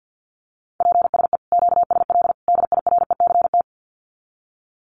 Morse code, heard as a keyed tone, about 49 wpm, 700 Hz. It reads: PSE QSL DIRECT
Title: Common QSO Phrases 50wpm